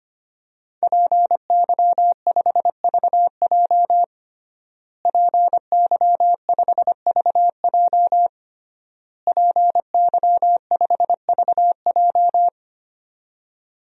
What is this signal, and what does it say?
Morse code, heard as a keyed tone, about 25 wpm, 700 Hz. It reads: PY5VJ PY5VJ PY5VJ
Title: DXCC-100 Call Signs - 5 Characters - Repeated 3x - 25wpm